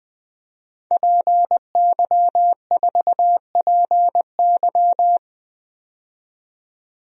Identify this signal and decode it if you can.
Morse code, heard as a keyed tone, about 20 wpm, 700 Hz. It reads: PY4PY
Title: DXCC-100 Call Signs - 5 Characters - 20wpm